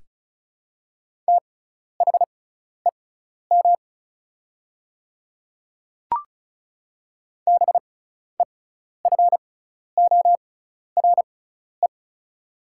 Morse code, heard as a keyed tone, about 36 wpm, spaced as for 12 wpm, 690 Hz. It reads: THEM BEFORE
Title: 6x Character Spacing - Top 200 Words - 35wpm